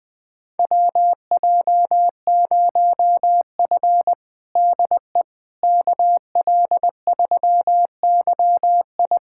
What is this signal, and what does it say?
Morse code, heard as a keyed tone, about 20 wpm, 700 Hz. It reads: WJ0F DE KL3YI
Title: USA Callsign DE Callsign 20wpm